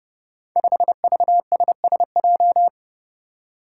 Morse code, heard as a keyed tone, about 30 wpm, 700 Hz. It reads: 5VSSJ
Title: Five Challenging Letter-Number Combinations 30wpm